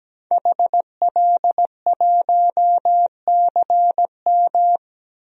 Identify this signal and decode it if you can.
Morse code, heard as a keyed tone, about 17 wpm, 700 Hz. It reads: HL1CM
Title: DXCC-100 Call Signs - 5 Characters - 17wpm